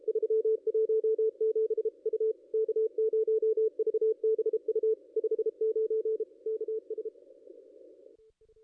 Morse code, heard as a keyed tone, about 34 wpm, 440 Hz. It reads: 317 U K0VBU 59 KS
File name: contests / ss-cw-k5tr-op-n5rz -0400